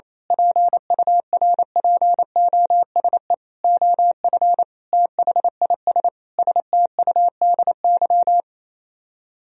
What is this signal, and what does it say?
Morse code, heard as a keyed tone, about 27 wpm, 700 Hz. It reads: PURPOSE OF THIS STUDY
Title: Speed-Racing - Sets of 5 Words 28wpm